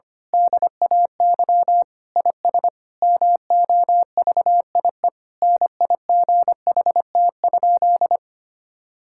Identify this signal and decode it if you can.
Morse code, heard as a keyed tone, about 25 wpm, 700 Hz. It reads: DAY IS MOVIE NIGHT?
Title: Sentences from Top 1000 Words - Review Part 5 - 25wpm